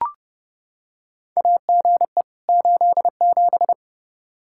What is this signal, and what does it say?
Morse code, heard as a keyed tone, about 30 wpm, 700 Hz. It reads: AGE 87
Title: K1USN SST QSO Practice 30wpm